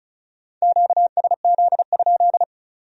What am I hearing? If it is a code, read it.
Morse code, heard as a keyed tone, about 36 wpm, 700 Hz. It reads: QSZ?
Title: Speed-Racing Q-Codes - 35wpm